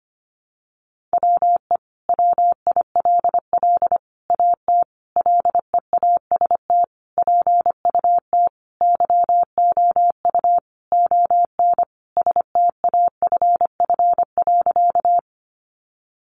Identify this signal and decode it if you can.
Morse code, heard as a keyed tone, about 25 wpm, 700 Hz. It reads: WE WILL AT LEAST PUT YOU ON STAFF.
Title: Sentences from Top 1200 Words 25wpm